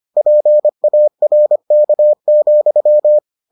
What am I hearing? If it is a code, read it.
Morse code, heard as a keyed tone, about 25 wpm, 600 Hz. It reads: PARK,